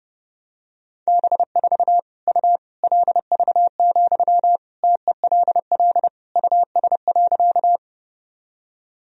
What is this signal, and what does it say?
Morse code, heard as a keyed tone, about 30 wpm, 700 Hz. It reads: B4 U LV, TELL US.
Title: Sentences from Top 400 Words and Abbreviations 30wpm